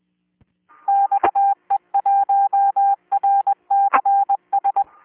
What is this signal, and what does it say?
Morse code, heard as a keyed tone, about 21 wpm, 800 Hz. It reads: XE1RCS